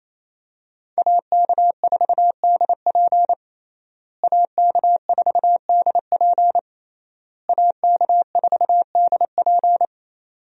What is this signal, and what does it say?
Morse code, heard as a keyed tone, about 28 wpm, 700 Hz. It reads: AK4DP AK4DP AK4DP
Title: USA Call Signs - 5 Characters - Repeated 3x - 28wpm